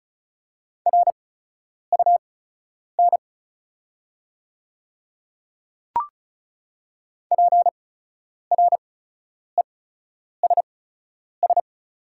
Morse code, heard as a keyed tone, about 35 wpm, spaced as for 10 wpm, 690 Hz. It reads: RUN PRESS
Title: 8x Character Spacing - Top 300 Words - 35wpm